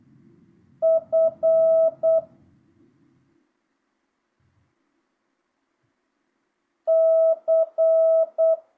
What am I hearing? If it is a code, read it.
Morse code, heard as a keyed tone, about 8 wpm, 650 Hz. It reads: F C
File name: jh8cbh / 2025cw